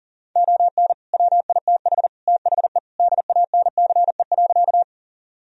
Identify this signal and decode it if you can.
Morse code, heard as a keyed tone, about 40 wpm, 700 Hz. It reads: ON WITH THE DANCE.
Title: Sentences from Top 1000 Words - Review Part 4 - 40wpm